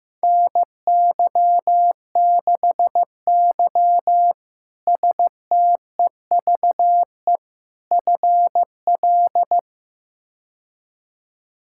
Morse code, heard as a keyed tone, about 15 wpm, 700 Hz. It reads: NY6Y STEVE FL
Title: K1USN SST QSO Practice - 15wpm